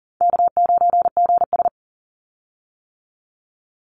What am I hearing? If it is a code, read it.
Morse code, heard as a keyed tone, about 41 wpm, 700 Hz. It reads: K9GS
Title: Call Signs - 40wpm